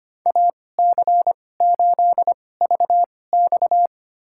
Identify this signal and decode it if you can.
Morse code, heard as a keyed tone, about 25 wpm, spaced as for 18 wpm, 700 Hz. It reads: AC8VX
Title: USA Call Signs - 5 Characters - 2x Character Spacing - 25wpm